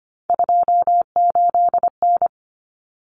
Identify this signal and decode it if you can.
Morse code, heard as a keyed tone, about 25 wpm, 700 Hz. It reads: 28N
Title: Common Cut Numbers - 3-Digit 25wpm